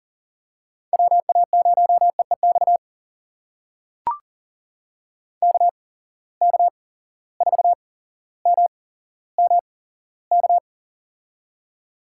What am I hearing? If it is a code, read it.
Morse code, heard as a keyed tone, about 40 wpm, spaced as for 17 wpm, 700 Hz. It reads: WA0EEX KK4MMK
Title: USA Call Signs - 6 Characters - 8x Character Spacing - 40wpm